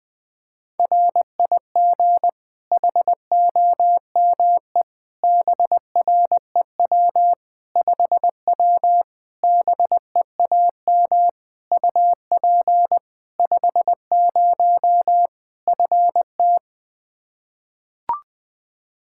Morse code, heard as a keyed tone, about 21 wpm, 700 Hz. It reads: RIG HOME BREW 5W BEAM UP 50 FT
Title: Practice QSO - Exchange 2 - Info - 20wpm